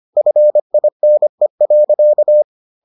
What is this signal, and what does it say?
Morse code, heard as a keyed tone, about 25 wpm, 600 Hz. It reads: FINE.